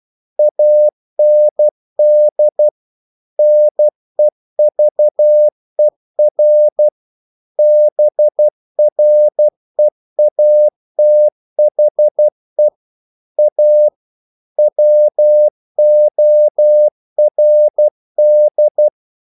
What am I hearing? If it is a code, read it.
Morse code, heard as a keyed tone, about 12 wpm, 600 Hz. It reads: AND NEVER BREATHE A WORD
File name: CW / Practice / If12